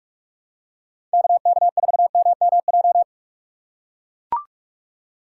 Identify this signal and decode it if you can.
Morse code, heard as a keyed tone, about 44 wpm, 690 Hz. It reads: KK4MMJ
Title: USA Call Signs - 4x Character Spacing - 45wpm